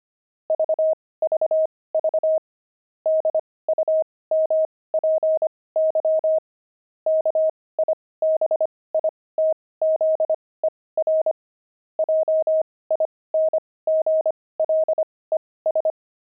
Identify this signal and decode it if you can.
Morse code, heard as a keyed tone, about 25 wpm, spaced as for 18 wpm, 630 Hz. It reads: VVV DUMPY KIBITZER JINGLES
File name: pangram-fs-18-cs-25-kibitzer